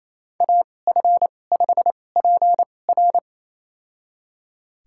Morse code, heard as a keyed tone, about 29 wpm, spaced as for 21 wpm, 700 Hz. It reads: AF5PR
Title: USA Call Signs - 5 Characters - 2x Character Spacing - 28wpm